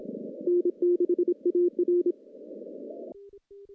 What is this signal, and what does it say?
Morse code, heard as a keyed tone, about 27 wpm, 350 Hz. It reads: N6AR
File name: contests / CW-SS-k5tr- -1400